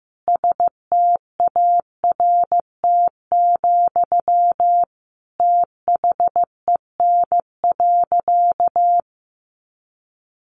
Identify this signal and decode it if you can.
Morse code, heard as a keyed tone, about 15 wpm, 700 Hz. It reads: START, THEN.
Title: Sentences from Top 300 Words 15wpm